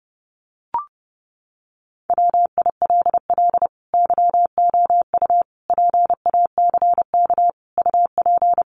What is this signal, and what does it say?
Morse code, heard as a keyed tone, about 30 wpm, 700 Hz. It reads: WILL YOU PACK UP
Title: Sentences from Top 1300 Words 30wpm